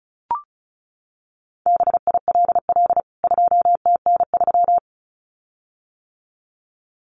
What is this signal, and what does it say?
Morse code, heard as a keyed tone, about 34 wpm, 700 Hz. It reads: BILL 2TN3
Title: CWT Exchange 35wpm